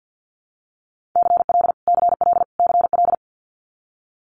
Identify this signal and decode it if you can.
Morse code, heard as a keyed tone, about 50 wpm, 700 Hz. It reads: CL CL CL
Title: Multi-letter Phonemes Repeated 3x 50wpm